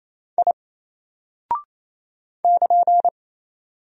Morse code, heard as a keyed tone, about 28 wpm, 700 Hz. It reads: I <KN>
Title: Single Character (Letter:Number:Punctuation:Prosign) 28wpm